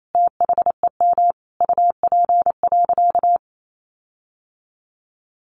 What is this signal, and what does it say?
Morse code, heard as a keyed tone, about 28 wpm, 700 Hz. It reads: THEM UP.
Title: Sentences from Top 1500 Words 28wpm